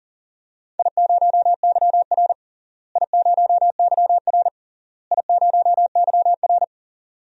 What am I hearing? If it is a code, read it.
Morse code, heard as a keyed tone, about 40 wpm, 700 Hz. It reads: I0YR I0YR I0YR
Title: DXCC-100 Call Signs - 4 Characters - Repeated 3x - 40wpm